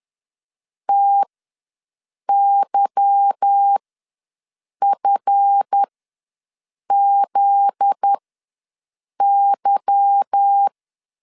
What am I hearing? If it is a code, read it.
Morse code, heard as a keyed tone, about 10 wpm, spaced as for 6 wpm, 800 Hz. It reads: TYFZY
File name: LEZIONI / 8 YJ45 / 8 Y ws 10wpm 3952